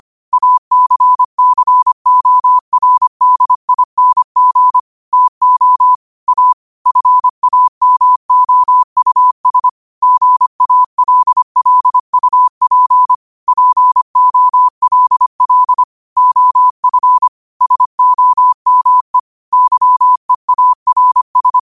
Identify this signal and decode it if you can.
Morse code, heard as a keyed tone, about 25 wpm, 1000 Hz. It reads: ACCORDING TO A FAMOUS GALLUP POLL OF SOME YEARS
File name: qotdfiles / 25wpm